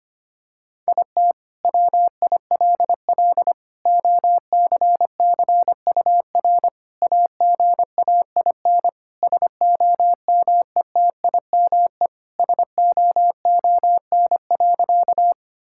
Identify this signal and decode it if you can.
Morse code, heard as a keyed tone, about 25 wpm, 700 Hz. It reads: IT WILL OCCUR AGAIN SOMETIME SOON.
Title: Sentences from Top 1000 Words - Review Part 6 - 25wpm